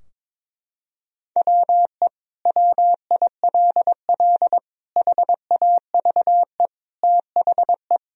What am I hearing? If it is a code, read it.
Morse code, heard as a keyed tone, about 22 wpm, 700 Hz. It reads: WE WILL HAVE THE
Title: Sentences from Top 1000 Words - Review Part 3 - 22wpm